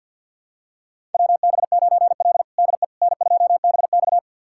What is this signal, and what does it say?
Morse code, heard as a keyed tone, about 50 wpm, 700 Hz. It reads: WB9L DE N2BX
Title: USA Callsign DE Callsign - 4 Characters - 50wpm